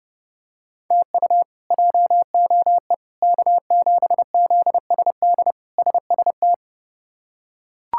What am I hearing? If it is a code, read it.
Morse code, heard as a keyed tone, about 30 wpm, 700 Hz. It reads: TU JOE K7ZSD SST
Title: K1USN SST QSO Practice - 30wpm